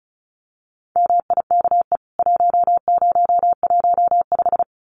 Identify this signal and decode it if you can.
Morse code, heard as a keyed tone, about 34 wpm, 700 Hz. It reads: MIKE 1015
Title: CWT Exchange 35wpm